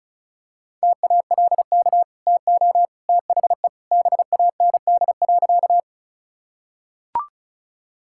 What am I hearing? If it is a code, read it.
Morse code, heard as a keyed tone, about 34 wpm, 700 Hz. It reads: TALK TO THE BAND.
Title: Sentences from Top 1600 Words 35wpm